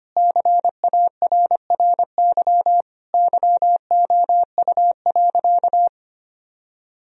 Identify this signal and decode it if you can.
Morse code, heard as a keyed tone, about 25 wpm, 700 Hz. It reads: CARRY YOU.
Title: Sentences from Top 1000 Words - Review Part 1 - 25wpm